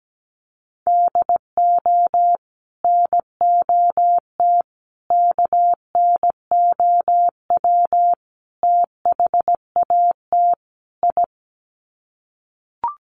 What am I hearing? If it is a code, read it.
Morse code, heard as a keyed tone, about 17 wpm, 700 Hz. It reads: DO NOT KNOW THAT I
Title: Sets of 5 Words Encore2 17wpm